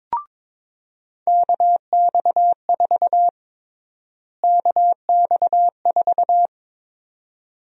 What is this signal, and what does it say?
Morse code, heard as a keyed tone, about 22 wpm, 700 Hz. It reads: KX4 KX4
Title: USA Call Sign Prefixes - Repeated 2x - 22wpm